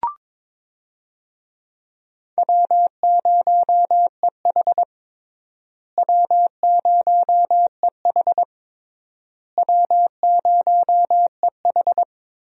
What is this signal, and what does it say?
Morse code, heard as a keyed tone, about 22 wpm, 700 Hz. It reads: W0EH W0EH W0EH
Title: USA Call Signs - 4 Characters - Repeated 3x - 22wpm